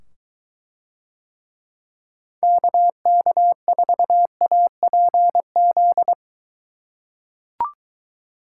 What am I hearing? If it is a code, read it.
Morse code, heard as a keyed tone, about 23 wpm, 700 Hz. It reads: KK4APZ